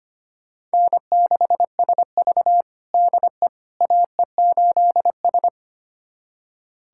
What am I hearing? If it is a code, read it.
Morse code, heard as a keyed tone, about 25 wpm, 700 Hz. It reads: N6SV DE AE8S
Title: USA Callsign DE Callsign - 4 Characters - 25wpm